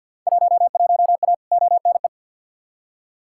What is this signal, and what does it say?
Morse code, heard as a keyed tone, about 50 wpm, 700 Hz. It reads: 11A ONE
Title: Field Day Exchanges Encore - 50wpm